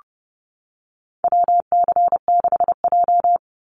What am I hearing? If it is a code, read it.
Morse code, heard as a keyed tone, about 30 wpm, 700 Hz. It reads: WC6J
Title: NA SOTA Activator Callsigns 30wpm